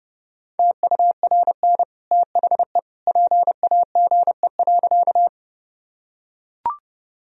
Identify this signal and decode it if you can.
Morse code, heard as a keyed tone, about 30 wpm, 700 Hz. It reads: TURN THE PAGE.
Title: Sentences from Top 1000 Words - Review Part 1 - 30wpm